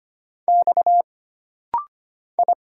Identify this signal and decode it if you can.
Morse code, heard as a keyed tone, about 25 wpm, 700 Hz. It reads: X I
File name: Single Character (Letter:Number:Punctuation:Prosign) - Rapid-Fire - 25wpm